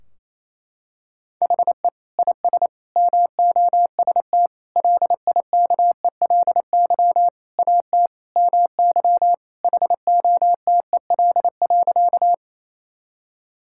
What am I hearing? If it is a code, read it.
Morse code, heard as a keyed tone, about 28 wpm, 700 Hz. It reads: HE IS MOST LIKELY AT MY HOTEL.
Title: Sentences from Top 1600 Words 28wpm